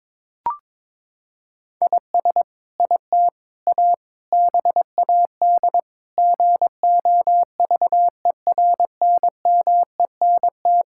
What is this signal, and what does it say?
Morse code, heard as a keyed tone, about 22 wpm, 700 Hz. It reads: IS IT A BAD GOVERNMENT
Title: Sentences from Top 1000 Words - Review Part 5 - 22wpm